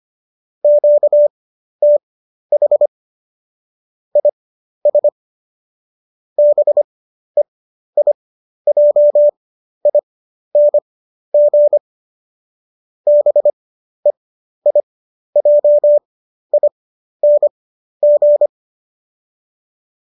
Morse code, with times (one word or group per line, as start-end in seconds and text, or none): 0.6-2.9 QTH
4.2-5.1 IS
6.4-11.8 BEIJING
13.1-18.5 BEIJING